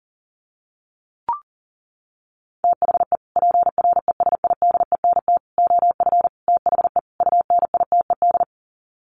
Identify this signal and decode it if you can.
Morse code, heard as a keyed tone, about 40 wpm, 700 Hz. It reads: THE PRESIDENT OF THE UNITED
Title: Sets of 5 Words 40wpm